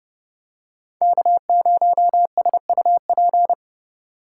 Morse code, heard as a keyed tone, about 29 wpm, 700 Hz. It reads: K0SUP